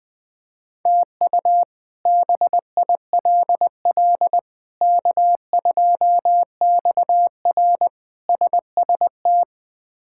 Morse code, heard as a keyed tone, about 20 wpm, 700 Hz. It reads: TU BILL K2XR SST
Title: K1USN SST QSO Practice - 20wpm